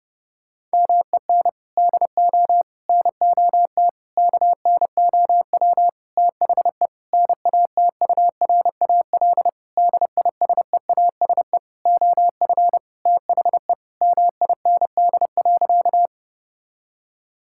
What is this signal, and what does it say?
Morse code, heard as a keyed tone, about 30 wpm, 700 Hz. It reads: MEN DO NOT KNOW THE NATURAL DISEASE OF THE MIND.
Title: Sentences from Top 1000 Words - Review Part 6 - 30wpm